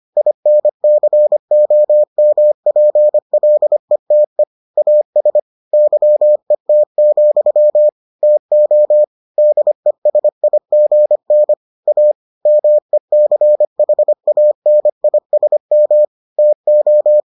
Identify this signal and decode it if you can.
Morse code, heard as a keyed tone, about 25 wpm, 600 Hz. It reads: INCOMPLETE AS YET, TO DESIGN A MECHANISM TO